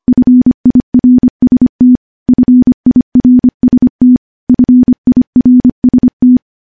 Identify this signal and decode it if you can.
Morse code, heard as a keyed tone, about 25 wpm, 250 Hz. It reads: FIRST FIRST FIRST
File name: morse-training-25WPM